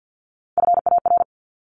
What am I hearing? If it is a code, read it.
Morse code, heard as a keyed tone, about 53 wpm, 700 Hz. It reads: FAR
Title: Top 300 Words Repeated 50wpm